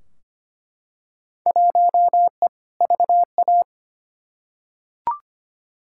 Morse code, heard as a keyed tone, about 25 wpm, 700 Hz. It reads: 1E VA
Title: Field Day Exchanges - 25wpm